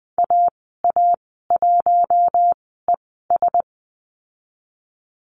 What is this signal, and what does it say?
Morse code, heard as a keyed tone, about 20 wpm, spaced as for 14 wpm, 700 Hz. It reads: AA1ES
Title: 048 Lesson 9g - Number 4 - Call Signs 20wpm